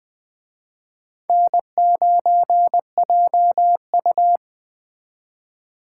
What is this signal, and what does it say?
Morse code, heard as a keyed tone, about 23 wpm, 700 Hz. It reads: N9JU
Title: Speed Racing - USA 1x2 Call Signs 20wpm